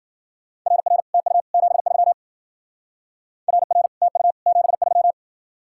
Morse code, heard as a keyed tone, about 49 wpm, 700 Hz. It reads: RR TU 73 RR TU 73
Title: Speed-Racing Common QSO Phrases Encore - 40wpm